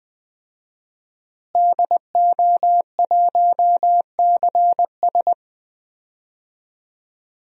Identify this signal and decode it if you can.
Morse code, heard as a keyed tone, about 21 wpm, 700 Hz. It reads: DO1CS
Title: DXCC-100 Call Signs - 5 Characters - Repeated 3x - 20wpm